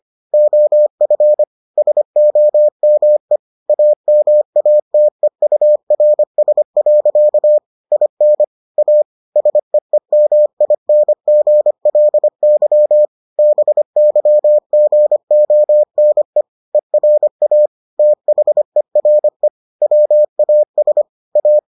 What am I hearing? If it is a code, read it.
Morse code, heard as a keyed tone, about 25 wpm, 600 Hz. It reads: OF SOME AMATEURS. IN A SEEMINGLY BYGONE ERA THERE WAS A